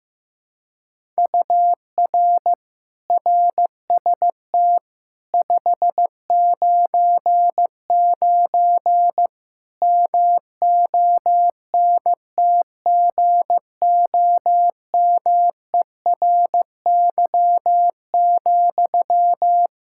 Morse code, heard as a keyed tone, about 15 wpm, 700 Hz. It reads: UR RST 599 MONTGOMERY,
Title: K1USN SST QSO Practice 15wpm